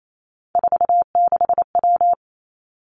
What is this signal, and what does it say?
Morse code, heard as a keyed tone, about 28 wpm, 690 Hz. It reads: 46W
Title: Three Challenging Letter-Number Combinations 28wpm